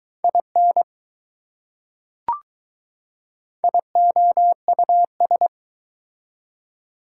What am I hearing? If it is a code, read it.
Morse code, heard as a keyed tone, about 20 wpm, 700 Hz. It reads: IN IOUS